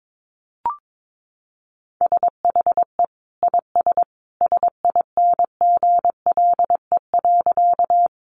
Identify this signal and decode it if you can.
Morse code, heard as a keyed tone, about 22 wpm, 700 Hz. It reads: SHE IS SINGLE.
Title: Sentences from Top 700 Words 22wpm